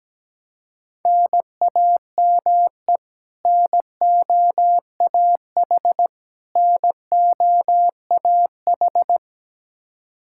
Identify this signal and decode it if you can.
Morse code, heard as a keyed tone, about 17 wpm, 700 Hz. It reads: NAME NOAH NOAH
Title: Practice QSO - Exchange 1 - Beginning - 17wpm